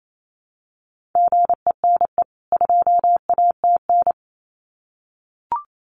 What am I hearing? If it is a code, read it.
Morse code, heard as a keyed tone, about 28 wpm, 700 Hz. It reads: GENE 2ATN
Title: CWT Exchange 28wpm